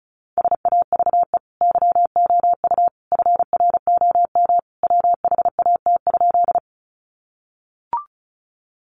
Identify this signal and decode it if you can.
Morse code, heard as a keyed tone, about 34 wpm, 700 Hz. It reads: SAVE YOU FROM WHAT?
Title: Sentences from Top 1100 Words 35wpm